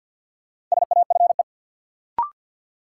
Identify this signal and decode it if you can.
Morse code, heard as a keyed tone, about 48 wpm, 700 Hz. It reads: SAFE
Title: Sentences from Top 900 Words - New Word Drill - 50wpm